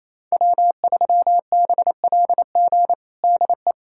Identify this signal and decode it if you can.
Morse code, heard as a keyed tone, about 27 wpm, 700 Hz. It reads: W3BLG DE
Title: USA Callsign DE Callsign - 5 Characters - 28wpm